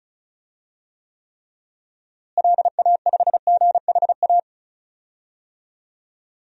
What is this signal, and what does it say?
Morse code, heard as a keyed tone, about 36 wpm, 700 Hz. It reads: LA5GHA
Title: DXCC-100 Call Signs - 6 Characters - Repeated 3x - 35wpm